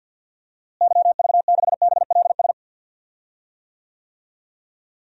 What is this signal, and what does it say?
Morse code, heard as a keyed tone, about 53 wpm, 700 Hz. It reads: YV6BLS
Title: DXCC-100 Call Signs - 6 Characters - Repeated 2x - 50wpm